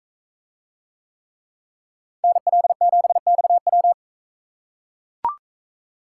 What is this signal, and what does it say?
Morse code, heard as a keyed tone, about 42 wpm, 700 Hz. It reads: NL7XW